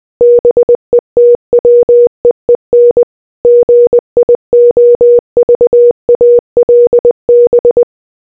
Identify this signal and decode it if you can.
Morse code, heard as a keyed tone, about 20 wpm, 480 Hz. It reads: BETWEEN GIOVALB